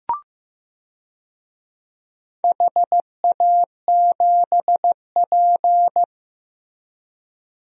Morse code, heard as a keyed tone, about 15 wpm, 700 Hz. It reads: HA7P
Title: DXCC-100 Call Signs - 4 Characters - Repeated 3x - 15wpm